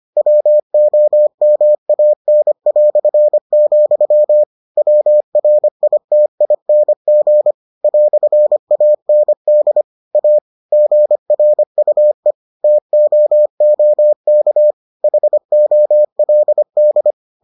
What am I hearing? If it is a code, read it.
Morse code, heard as a keyed tone, about 25 wpm, 600 Hz. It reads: WOMAN", WRITING "AND A GRUE TOOK HOLD